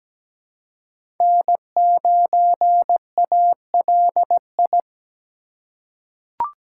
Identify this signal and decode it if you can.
Morse code, heard as a keyed tone, about 17 wpm, 700 Hz. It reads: N9ALI
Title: USA Call Signs - 5 Characters - 4x Character Spacing - 17wpm